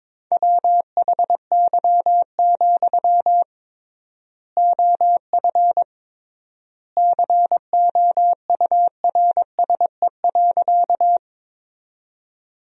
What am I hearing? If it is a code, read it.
Morse code, heard as a keyed tone, about 22 wpm, 700 Hz. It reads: WHY, OF COURSE.
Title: 3x Word-Spacing - Sentences from Top 500 Words - 22wpm